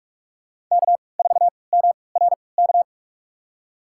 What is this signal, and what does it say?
Morse code, heard as a keyed tone, about 46 wpm, spaced as for 26 wpm, 700 Hz. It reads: K4MRK
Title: USA Call Signs - 5 Characters - 3x Character Spacing - 45wpm